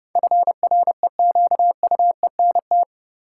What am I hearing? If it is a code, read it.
Morse code, heard as a keyed tone, about 30 wpm, 700 Hz. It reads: FREQUENT
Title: Sentences from Top 1400 Words - New Word Drill Alt - 30wpm